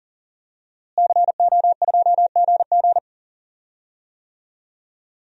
Morse code, heard as a keyed tone, about 39 wpm, 700 Hz. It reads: CO2GG
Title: DXCC-100 Call Signs - 5 Characters - Repeated 3x - 40wpm